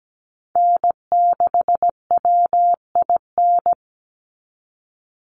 Morse code, heard as a keyed tone, about 17 wpm, 700 Hz. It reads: N6WIN
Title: Call Signs Repeated 2x - 17wpm